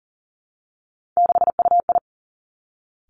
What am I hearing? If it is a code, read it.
Morse code, heard as a keyed tone, about 38 wpm, 700 Hz. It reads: 6UI
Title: Three Challenging Letter-Number Combinations 40wpm